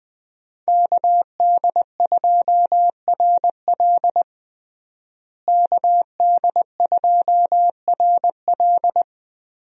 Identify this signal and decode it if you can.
Morse code, heard as a keyed tone, about 20 wpm, 700 Hz. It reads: KD2RL KD2RL
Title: USA Call Signs - 5 Characters - Repeated 2x - 20wpm